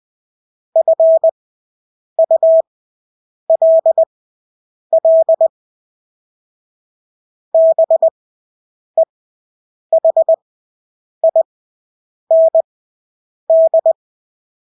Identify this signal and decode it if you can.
Morse code, heard as a keyed tone, about 20 wpm, spaced as for 8 wpm, 650 Hz. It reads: FULL BEHIND
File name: Engelske ord 20/8 wpm